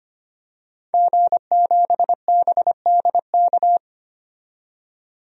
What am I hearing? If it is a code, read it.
Morse code, heard as a keyed tone, about 25 wpm, 700 Hz. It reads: G7BDK
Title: DXCC-100 Call Signs - 5 Characters - 25wpm